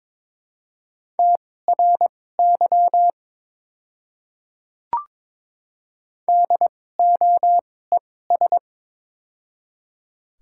Morse code, heard as a keyed tone, about 22 wpm, spaced as for 16 wpm, 700 Hz. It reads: TRY DOES
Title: 2x Character Spacing - Top 200 Words - 22wpm